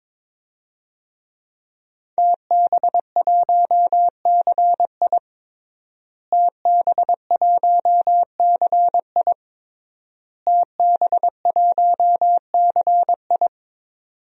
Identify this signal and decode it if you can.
Morse code, heard as a keyed tone, about 22 wpm, 700 Hz. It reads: TB1CI TB1CI TB1CI
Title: DXCC-100 Call Signs - 5 Characters - Repeated 3x - 22wpm